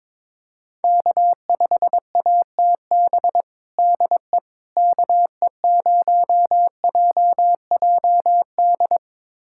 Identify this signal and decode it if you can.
Morse code, heard as a keyed tone, about 22 wpm, 700 Hz. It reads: K5ATB DE KE0JJD
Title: USA Callsign DE Callsign 22wpm